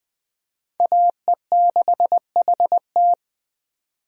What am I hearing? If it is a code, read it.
Morse code, heard as a keyed tone, about 20 wpm, 700 Hz. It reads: AE6HT
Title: USA Call Signs - 5 Characters - 20wpm